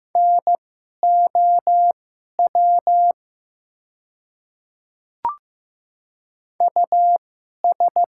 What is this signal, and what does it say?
Morse code, heard as a keyed tone, about 15 wpm, spaced as for 11 wpm, 700 Hz. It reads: NOW US
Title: 2x Character Spacing - Top 200 Words - 15wpm